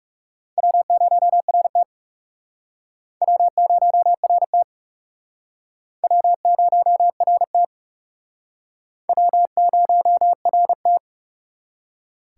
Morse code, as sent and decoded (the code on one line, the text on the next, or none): .-- ----- .-. - / .-- ----- .-. - / .-- ----- .-. - / .-- ----- .-. -
W0RT W0RT W0RT W0RT